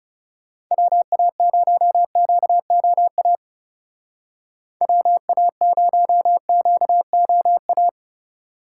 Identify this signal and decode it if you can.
Morse code, heard as a keyed tone, about 35 wpm, 700 Hz. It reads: WA0QOA WA0QOA
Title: Speed Racing - USA 2x3 Call Signs 30wpm